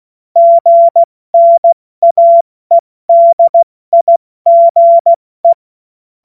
Morse code, heard as a keyed tone, about 16 wpm, 680 Hz. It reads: GNAEDIGE
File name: Morsekurs / Witze / Witz 09 Januar 2024 70 F80